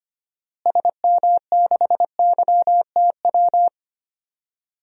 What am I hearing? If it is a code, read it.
Morse code, heard as a keyed tone, about 25 wpm, 700 Hz. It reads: SM6YTW
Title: DXCC-100 Call Signs - 6 Characters - Repeated 2x - 25wpm